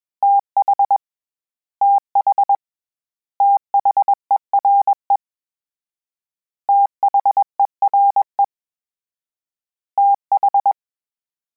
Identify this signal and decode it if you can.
Morse code, heard as a keyed tone, about 21 wpm, 800 Hz. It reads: TH TH THERE THERE TH
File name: Combinations / Combinations-mp3s / Combinations@20